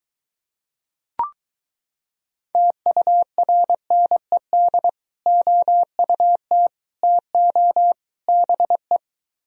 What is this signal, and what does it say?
Morse code, heard as a keyed tone, about 23 wpm, 700 Hz. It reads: TURNED OUT TO BE